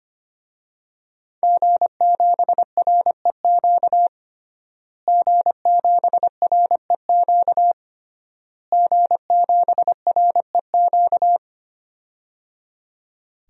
Morse code, heard as a keyed tone, about 25 wpm, 700 Hz. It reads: G7REQ G7REQ G7REQ
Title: DXCC-100 Call Signs - 5 Characters - Repeated 3x - 25wpm